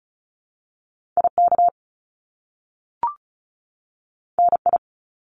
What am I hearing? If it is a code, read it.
Morse code, heard as a keyed tone, about 36 wpm, 700 Hz. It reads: IK NI
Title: Top Two Letter Combinations in Words 35wpm